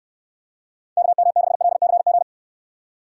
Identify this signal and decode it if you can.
Morse code, heard as a keyed tone, about 69 wpm, 700 Hz. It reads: KM6LCG